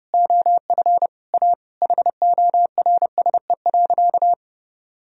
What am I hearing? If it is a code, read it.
Morse code, heard as a keyed tone, about 32 wpm, 700 Hz. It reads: OF A HORSE.
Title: Sentences from Top 1000 Words - Review Part 4 - 30wpm